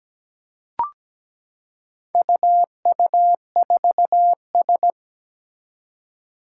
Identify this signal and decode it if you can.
Morse code, heard as a keyed tone, about 17 wpm, 700 Hz. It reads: UU4S
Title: Four Challenging Letter-Number Combinations 17wpm